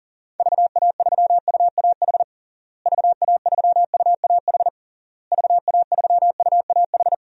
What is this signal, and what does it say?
Morse code, heard as a keyed tone, about 37 wpm, 700 Hz. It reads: VA3UAH VA3UAH VA3UAH
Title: DXCC-100 Call Signs - 6 Characters - Repeated 3x - 40wpm